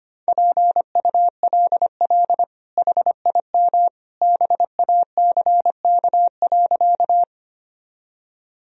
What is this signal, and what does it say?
Morse code, heard as a keyed tone, about 25 wpm, 700 Hz. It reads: PULL HIM BACK.
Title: Sentences from Top 1000 Words - Review Part 2 - 25wpm